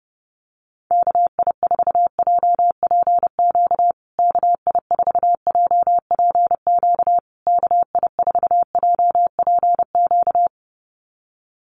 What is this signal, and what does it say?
Morse code, heard as a keyed tone, about 30 wpm, 700 Hz. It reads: KI4JPQ KI4JPQ KI4JPQ
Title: USA Call Signs Repeated 3x 30wpm